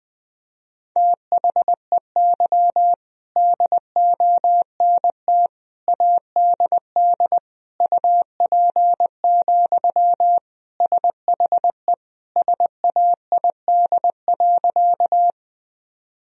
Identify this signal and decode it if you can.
Morse code, heard as a keyed tone, about 20 wpm, 700 Hz. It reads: THEY DONT ADD UP, SHE SAID.
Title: Sentences from Top 1000 Words - Review Part 1 - 20wpm